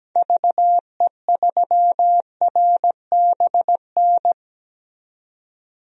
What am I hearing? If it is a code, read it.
Morse code, heard as a keyed tone, about 17 wpm, 700 Hz. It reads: VE3RBN
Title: DXCC-100 Call Signs - 6 Characters - 17wpm